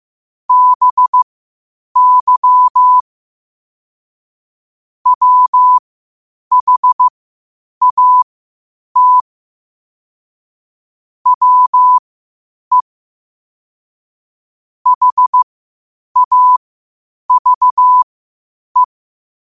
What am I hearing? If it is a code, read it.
Morse code, heard as a keyed tone, about 15 wpm, spaced as for 8 wpm, 1000 Hz. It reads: BY WHAT WE HAVE
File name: qotdfiles / 8wpm